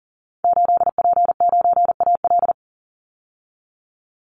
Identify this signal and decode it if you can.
Morse code, heard as a keyed tone, about 40 wpm, 700 Hz. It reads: 8P9AL
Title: Call Signs Repeated 3x - 40wpm